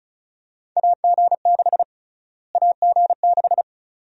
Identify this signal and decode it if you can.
Morse code, heard as a keyed tone, about 34 wpm, 700 Hz. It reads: AG6 AG6
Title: USA Call Sign Prefixes - Repeated 2x - 35wpm